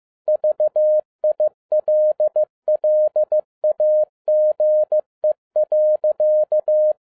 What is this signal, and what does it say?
Morse code, heard as a keyed tone, about 15 wpm, 610 Hz. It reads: VILLAGE.
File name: 2011-10-mp3 / n02-FIELD DAY CONTEST QRV FROM SAKAE VILLAGE 15wpm